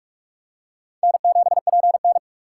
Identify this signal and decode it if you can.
Morse code, heard as a keyed tone, about 47 wpm, 700 Hz. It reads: N7PN
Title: USA Call Signs - 4 Characters - 4x Character Spacing - 45wpm